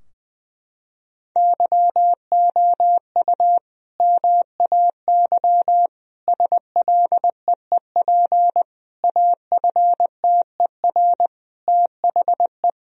Morse code, heard as a keyed tone, about 20 wpm, 700 Hz. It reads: YOU MAY SLEEP AFTER THE
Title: Sentences from Top 1000 Words - Review Part 4 - 20wpm